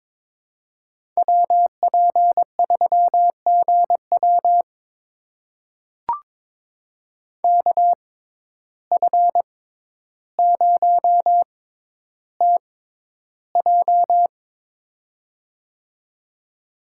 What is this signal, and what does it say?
Morse code, heard as a keyed tone, about 22 wpm, spaced as for 11 wpm, 700 Hz. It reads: WP3GW KF0TJ
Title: USA Call Signs - 5 Characters - 6x Character Spacing - 22wpm